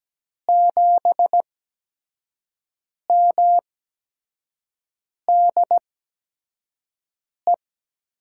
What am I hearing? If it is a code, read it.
Morse code, heard as a keyed tone, about 17 wpm, spaced as for 5 wpm, 700 Hz. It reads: 7MDE
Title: USA Call Signs - 6 Characters - 8x Character Spacing - 17wpm